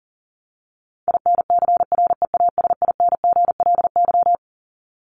Morse code, heard as a keyed tone, about 40 wpm, 700 Hz. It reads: INCREASINGLY
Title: LY Suffix - Top 100 Words with Base Words 40wpm